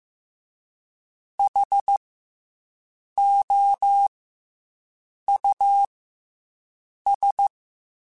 Morse code, heard as a keyed tone, about 15 wpm, 800 Hz. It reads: H O U S
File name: Special INGs / Special INGs-mp3s / SlowToFaster ing @13